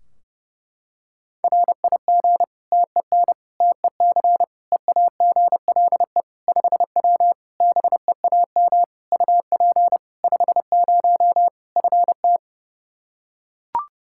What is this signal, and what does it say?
Morse code, heard as a keyed tone, about 30 wpm, 700 Hz. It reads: RIG TEN TEC EAGLE 5W BEAM UP 50 FT
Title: Practice QSO - Exchange 2 - Info - 30wpm